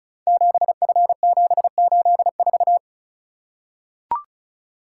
Morse code, heard as a keyed tone, about 36 wpm, 700 Hz. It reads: 7F784
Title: Code Groups - 5 Letter:Numbers - 35wpm